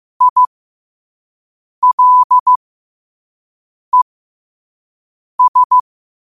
Morse code, heard as a keyed tone, about 15 wpm, spaced as for 5 wpm, 1000 Hz. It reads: ILES
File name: qotdfiles / 5wpm